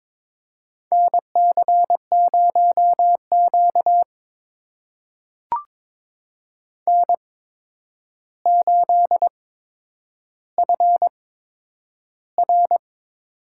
Morse code, heard as a keyed tone, about 22 wpm, spaced as for 9 wpm, 700 Hz. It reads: NC0Q N8FR
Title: USA Call Signs - 4 Characters - 8x Character Spacing - 22wpm